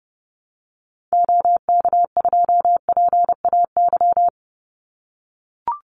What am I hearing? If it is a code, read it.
Morse code, heard as a keyed tone, about 30 wpm, 700 Hz. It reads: OK2PAY
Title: Call Signs - 30wpm